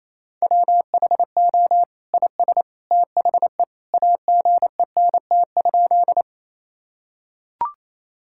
Morse code, heard as a keyed tone, about 28 wpm, 700 Hz. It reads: WHO IS THE AGENT?
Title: Sentences from Top 1600 Words 28wpm